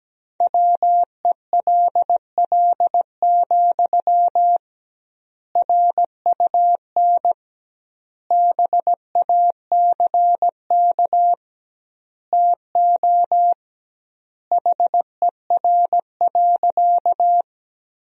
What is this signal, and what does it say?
Morse code, heard as a keyed tone, about 17 wpm, 700 Hz. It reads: WELL, RUN BACK TO HER.
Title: 2x Word-Spacing - Sentences from Top 300 Words - 17wpm